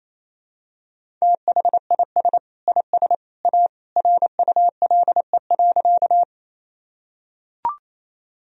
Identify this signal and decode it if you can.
Morse code, heard as a keyed tone, about 28 wpm, 700 Hz. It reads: THIS IS A RULE.
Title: Sentences from Top 1000 Words - Review Part 2 - 28wpm